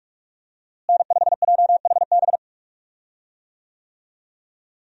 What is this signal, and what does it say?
Morse code, heard as a keyed tone, about 46 wpm, 700 Hz. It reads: N5JHB
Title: USA Call Signs - 5 Characters - Repeated 3x - 45wpm